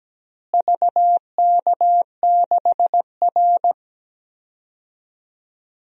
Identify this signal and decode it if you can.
Morse code, heard as a keyed tone, about 17 wpm, 700 Hz. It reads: VK6R
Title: DXCC-100 Call Signs - 4 Characters - Repeated 2x - 17wpm